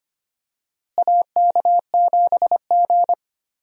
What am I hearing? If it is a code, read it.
Morse code, heard as a keyed tone, about 25 wpm, 700 Hz. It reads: AK7G
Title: USA Call Signs - 4 Characters - Repeated 2x - 25wpm